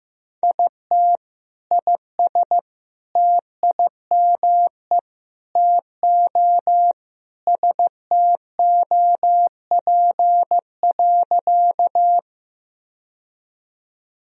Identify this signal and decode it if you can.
Morse code, heard as a keyed tone, about 15 wpm, 700 Hz. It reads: IT IS TIME TO STOP.
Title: Sentences from Top 1000 Words - Review Part 1 - 15wpm